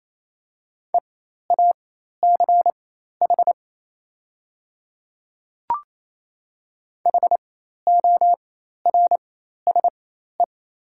Morse code, heard as a keyed tone, about 27 wpm, spaced as for 13 wpm, 700 Hz. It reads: EACH HORSE
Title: 4x Character Spacing - Top 300 Words - 28wpm